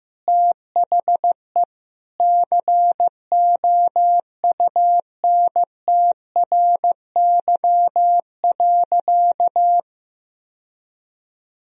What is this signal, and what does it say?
Morse code, heard as a keyed tone, about 15 wpm, 700 Hz. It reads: THE COUNTRY.
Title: Sentences from Top 1000 Words - Review Part 2 - 15wpm